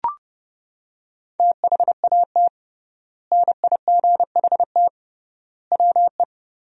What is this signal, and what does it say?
Morse code, heard as a keyed tone, about 29 wpm, 700 Hz. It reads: THAT NIGHT WE
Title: 3x Word-Spacing - Sentences from Top 300 Words - 30wpm